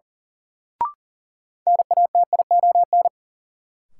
Morse code, heard as a keyed tone, about 37 wpm, 700 Hz. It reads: NATION
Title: Sets of 3 Words - New Word Drill - 40wpm